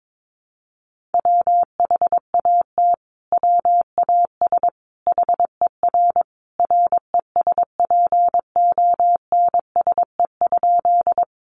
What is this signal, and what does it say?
Morse code, heard as a keyed tone, about 22 wpm, 700 Hz. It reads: WHAT WAS HER RESPONSE?
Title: Sentences from Top 1400 Words 22wpm